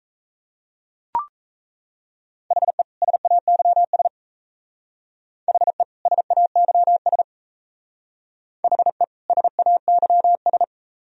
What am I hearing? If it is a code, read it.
Morse code, heard as a keyed tone, about 36 wpm, 700 Hz. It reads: HE SAYS HE SAYS HE SAYS